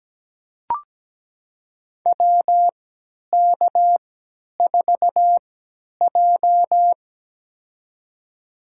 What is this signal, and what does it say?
Morse code, heard as a keyed tone, about 17 wpm, spaced as for 10 wpm, 700 Hz. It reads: WK4J
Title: USA Call Signs - 4 Characters - 3x Character Spacing - 17wpm